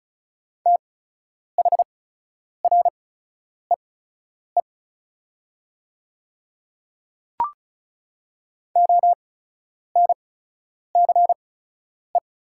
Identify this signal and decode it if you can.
Morse code, heard as a keyed tone, about 34 wpm, spaced as for 10 wpm, 700 Hz. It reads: THREE ONCE
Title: 8x Character Spacing - Top 300 Words - 35wpm